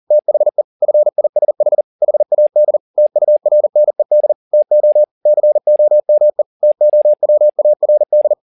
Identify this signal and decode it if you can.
Morse code, heard as a keyed tone, about 40 wpm, 600 Hz. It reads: THE FISH HAD TURNED TO COME TOWARD
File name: the old man and the sea 2 40wpm